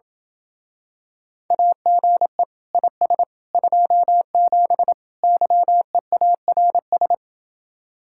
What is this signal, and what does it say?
Morse code, heard as a keyed tone, about 27 wpm, 700 Hz. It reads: AGE IS 27 YEARS